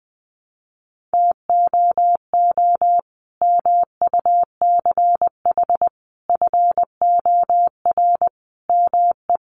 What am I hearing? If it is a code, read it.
Morse code, heard as a keyed tone, about 20 wpm, 700 Hz. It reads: TOO MUCH FOR ME
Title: Sentences from Top 200 Words 20wpm